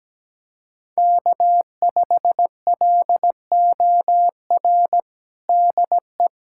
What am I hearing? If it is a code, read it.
Morse code, heard as a keyed tone, about 17 wpm, 700 Hz. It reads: K5LOR DE
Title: USA Callsign DE Callsign 17wpm